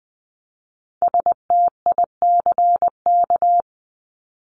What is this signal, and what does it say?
Morse code, heard as a keyed tone, about 20 wpm, 700 Hz. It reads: STICK
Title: Top 1001-1250 Words 20wpm